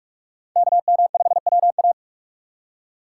Morse code, heard as a keyed tone, about 44 wpm, 700 Hz. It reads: KM5WA
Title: USA Call Signs - 5 Characters - 4x Character Spacing - 45wpm